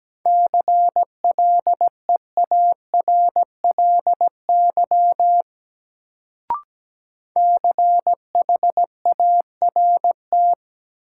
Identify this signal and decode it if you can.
Morse code, heard as a keyed tone, about 17 wpm, 700 Hz. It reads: CLEARLY CHART
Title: Sentences from Top 1300 Words - New Word Drill - 17wpm